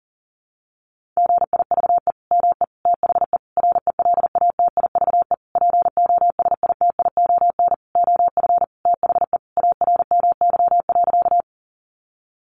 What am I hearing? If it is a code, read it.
Morse code, heard as a keyed tone, about 40 wpm, 700 Hz. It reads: GIVE ME THE RELATIVE POSITION OF THE ARMY.
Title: Sentences from Top 1400 Words 40wpm